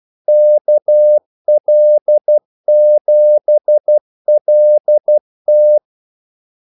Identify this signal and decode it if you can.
Morse code, heard as a keyed tone, about 12 wpm, 600 Hz. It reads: KL7LT
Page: 2 ekstra ordmellemrum mellem kaldesignalerne